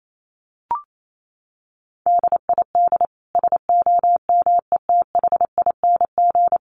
Morse code, heard as a keyed tone, about 28 wpm, 700 Hz. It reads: DID SOMETHING
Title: Sentences from Top 600 Words 28wpm